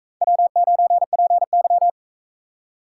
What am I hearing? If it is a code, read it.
Morse code, heard as a keyed tone, about 40 wpm, 700 Hz. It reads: W9PY